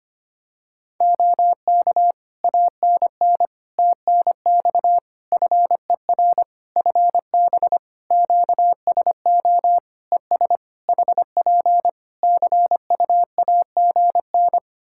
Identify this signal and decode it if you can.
Morse code, heard as a keyed tone, about 25 wpm, 700 Hz. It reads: OK ANN TNX FER FB QSO ES HP CUAGN
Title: K1USN SST QSO Practice 25wpm